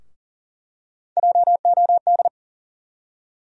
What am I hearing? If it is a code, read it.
Morse code, heard as a keyed tone, about 40 wpm, 700 Hz. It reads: JOD
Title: USA Call Sign Postfixes - 3 Letters 40wpm